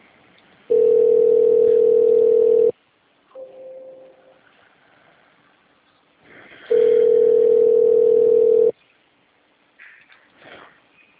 Ringback tone (North America), 440 + 480 Hz.